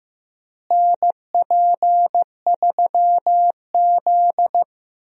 Morse code, heard as a keyed tone, about 15 wpm, 700 Hz. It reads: NP3Z
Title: USA Call Signs - 4 Characters - Repeated 3x - 15wpm